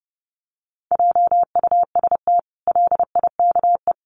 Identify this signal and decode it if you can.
Morse code, heard as a keyed tone, about 30 wpm, 700 Hz. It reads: JUST LIKE
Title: Sentences from Top 1200 Words 30wpm